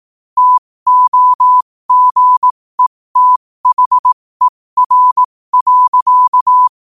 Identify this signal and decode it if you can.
Morse code, heard as a keyed tone, about 18 wpm, spaced as for 15 wpm, 1000 Hz. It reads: TOGETHER.
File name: qotdfiles / 15wpm